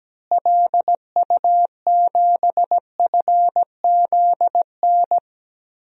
Morse code, heard as a keyed tone, about 17 wpm, 700 Hz. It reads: LU7FZN
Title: DXCC-100 Call Signs - 6 Characters - Repeated 2x - 17wpm